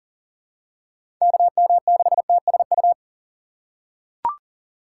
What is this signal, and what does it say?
Morse code, heard as a keyed tone, about 40 wpm, 700 Hz. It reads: KM6TSU
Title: USA Call Signs - 4x Character Spacing - 40wpm